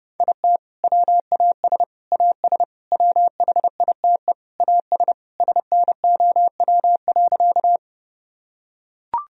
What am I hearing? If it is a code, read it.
Morse code, heard as a keyed tone, about 30 wpm, 700 Hz. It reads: IT WAS AS WHITE AS SNOW.
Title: Sentences from Top 1000 Words - Review Part 3 - 30wpm